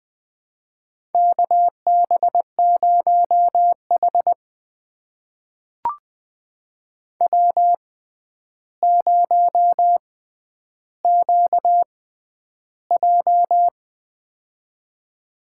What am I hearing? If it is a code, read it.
Morse code, heard as a keyed tone, about 20 wpm, spaced as for 10 wpm, 700 Hz. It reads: KB0H W0QJ
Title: USA Call Signs - 4 Characters - 6x Character Spacing - 20wpm